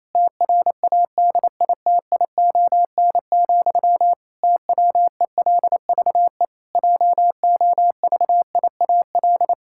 Morse code, heard as a keyed tone, about 28 wpm, 700 Hz. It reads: TRADITION, TWELVE JOVIAL
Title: Pangram Challenge v2 - 28wpm